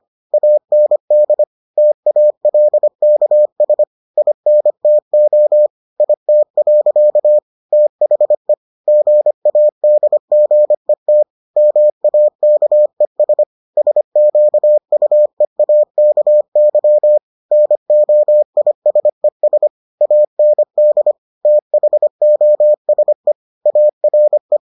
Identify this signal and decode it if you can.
Morse code, heard as a keyed tone, about 25 wpm, 600 Hz. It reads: AND TALKS INTO IT. THE GADGET MAKES SQUEAKY NOISES AND THOSE ARE